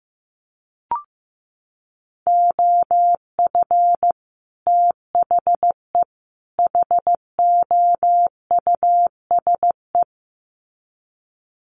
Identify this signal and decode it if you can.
Morse code, heard as a keyed tone, about 15 wpm, 700 Hz. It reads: OF THE HOUSE
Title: Sets of 3 Words 15wpm